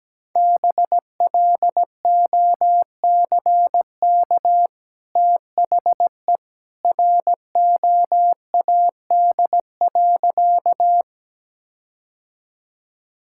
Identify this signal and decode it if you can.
Morse code, heard as a keyed tone, about 17 wpm, 700 Hz. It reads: BLOCK THE ROAD.
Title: Sentences from Top 1600 Words 17wpm